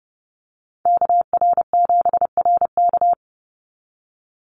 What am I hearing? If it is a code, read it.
Morse code, heard as a keyed tone, about 30 wpm, 700 Hz. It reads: KR7RK
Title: NA SOTA Activator Callsigns 30wpm